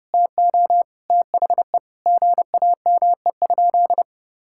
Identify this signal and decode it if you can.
Morse code, heard as a keyed tone, about 32 wpm, 700 Hz. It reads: TO THE GAME?
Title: Sentences from Top 1000 Words - Review Part 6 - 30wpm